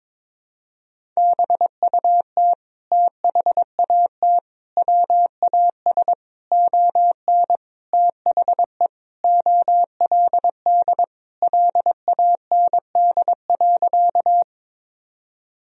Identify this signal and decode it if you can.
Morse code, heard as a keyed tone, about 21 wpm, 700 Hz. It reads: BUT THAT WAS ON THE OLD LAND.
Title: Sentences from Top 1000 Words - Review Part 1 - 22wpm